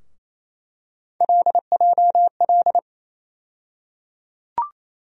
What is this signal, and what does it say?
Morse code, heard as a keyed tone, about 28 wpm, 700 Hz. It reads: LJL
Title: USA Call Sign Postfixes - 3 Letters 28wpm